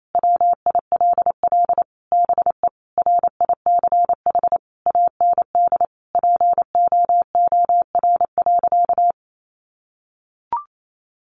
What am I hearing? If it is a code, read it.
Morse code, heard as a keyed tone, about 28 wpm, 700 Hz. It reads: WILL BE RICH AND POOR.
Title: Sentences from Top 1100 Words 28wpm